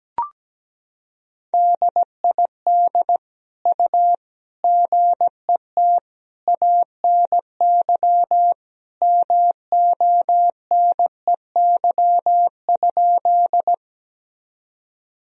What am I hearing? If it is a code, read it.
Morse code, heard as a keyed tone, about 17 wpm, 700 Hz. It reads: DID U GET ANY MONEY?
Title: Sentences from Top 400 Words and Abbreviations 17wpm